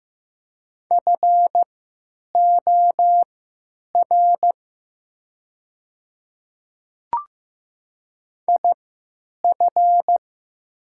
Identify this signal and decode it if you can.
Morse code, heard as a keyed tone, about 15 wpm, spaced as for 9 wpm, 700 Hz. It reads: FOR IF
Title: 3x Character Spacing - Top 100 Words - 15wpm